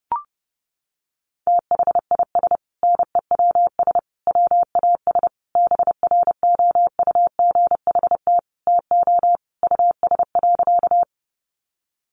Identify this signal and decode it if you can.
Morse code, heard as a keyed tone, about 30 wpm, 700 Hz. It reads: THIS NEWS WAS BROUGHT TO US.
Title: Sentences from Top 800 Words 30wpm